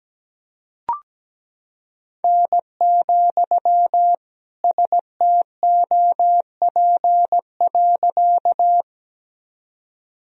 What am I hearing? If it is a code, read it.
Morse code, heard as a keyed tone, about 17 wpm, 700 Hz. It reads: N, STOP.
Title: Sentences from Top 300 Words and Abbreviations 17wpm